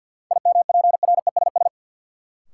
Morse code, heard as a keyed tone, about 48 wpm, 700 Hz. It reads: IMPRESS
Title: Sentences from Top 1600 Words - New Word Drill - 50wpm